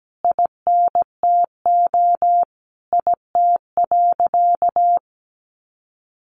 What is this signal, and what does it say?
Morse code, heard as a keyed tone, about 17 wpm, 700 Hz. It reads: INTO IT.
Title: Sentences from Top 1100 Words 17wpm